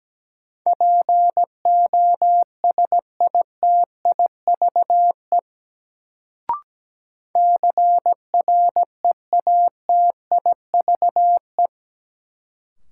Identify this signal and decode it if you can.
Morse code, heard as a keyed tone, about 17 wpm, 700 Hz. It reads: POSITIVE CREATIVE
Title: Sentences from Top 1000 Words - Review Part 5 - New Word Drill - 17wpm